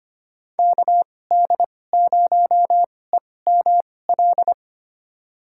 Morse code, heard as a keyed tone, about 25 wpm, spaced as for 18 wpm, 700 Hz. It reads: KD0EML
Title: USA Call Signs - 6 Characters - 2x Character Spacing - 25wpm